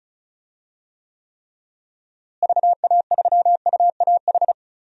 Morse code, heard as a keyed tone, about 34 wpm, 700 Hz. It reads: VA3UAH
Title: DXCC-100 Call Signs - 6 Characters - Repeated 3x - 35wpm